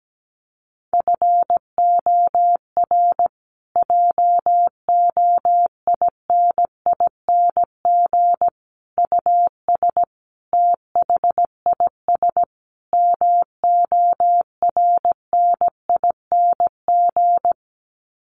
Morse code, heard as a keyed tone, about 17 wpm, 700 Hz. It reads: FOR JOINING US THIS MORNING
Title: Sets of 5 Words Encore 17wpm